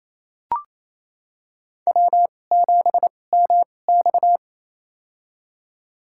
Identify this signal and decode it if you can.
Morse code, heard as a keyed tone, about 28 wpm, spaced as for 20 wpm, 700 Hz. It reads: W7MX
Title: USA Call Signs - 4 Characters - 2x Character Spacing - 28wpm